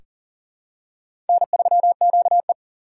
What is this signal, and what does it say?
Morse code, heard as a keyed tone, about 38 wpm, 700 Hz. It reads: N3QE
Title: K1USN SST QSO Practice - 40wpm